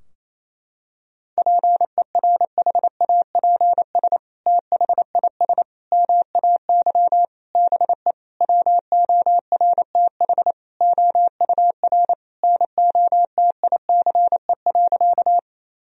Morse code, heard as a keyed tone, about 28 wpm, 690 Hz. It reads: PERHAPS THIS MAY BE WORTH OUR NOTICE.
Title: Sentences from Top 1000 Words - Review Part 2 - 28wpm